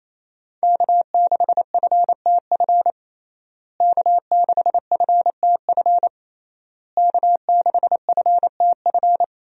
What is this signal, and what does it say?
Morse code, heard as a keyed tone, about 28 wpm, 700 Hz. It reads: K6FTF K6FTF K6FTF
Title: USA Call Signs - 5 Characters - Repeated 3x - 28wpm